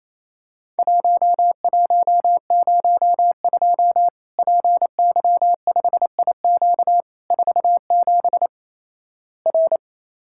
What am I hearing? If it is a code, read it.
Morse code, heard as a keyed tone, about 28 wpm, 700 Hz. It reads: 1102 PY5IQ 47 R
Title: Worked All Europe DX Contest - QTC Exchange - 28wpm